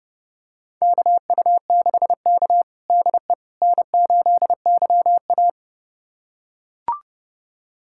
Morse code, heard as a keyed tone, about 29 wpm, 700 Hz. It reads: KU6K DE N8YA
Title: USA Callsign DE Callsign - 4 Characters - 30wpm